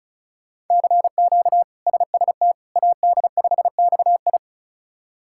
Morse code, heard as a keyed tone, about 36 wpm, 700 Hz. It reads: CQ SST AD5XI
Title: K1USN SST QSO Practice - 35wpm